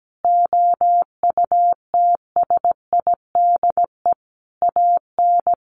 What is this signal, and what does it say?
Morse code, heard as a keyed tone, about 17 wpm, 700 Hz. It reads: OUTSIDE AN
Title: Sentences from Top 1100 Words 17wpm